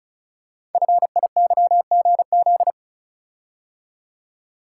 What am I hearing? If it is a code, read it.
Morse code, heard as a keyed tone, about 36 wpm, 700 Hz. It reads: FIYGZ
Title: Code Groups - 5 Letters - 35wpm